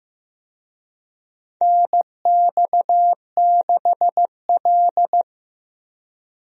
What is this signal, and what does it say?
Morse code, heard as a keyed tone, about 16 wpm, 700 Hz. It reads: NX6L
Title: USA Call Signs - 4 Characters - Repeated 2x - 15wpm